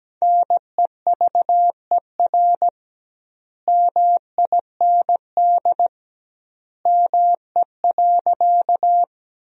Morse code, heard as a keyed tone, about 17 wpm, 700 Hz. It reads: NEVER MIND ME.
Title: 2x Word-Spacing - Sentences from Top 500 Words - 17wpm